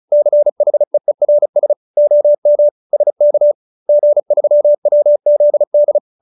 Morse code, heard as a keyed tone, about 34 wpm, 590 Hz. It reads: CHEERS OM SK G3WZD
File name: QSO310 35